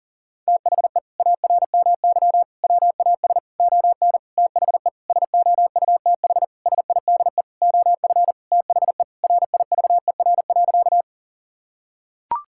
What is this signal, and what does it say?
Morse code, heard as a keyed tone, about 37 wpm, 700 Hz. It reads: THE ARMY WAS ON THE SOUTH SIDE OF THE RIVER.
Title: Sentences from Top 1000 Words - Review Part 5 - 40wpm